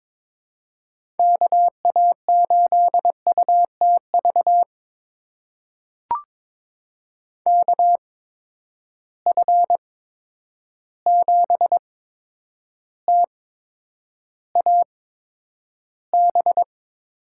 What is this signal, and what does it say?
Morse code, heard as a keyed tone, about 22 wpm, spaced as for 9 wpm, 700 Hz. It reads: KA8UTV KF7TAB
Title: USA Call Signs - 6 Characters - 8x Character Spacing - 22wpm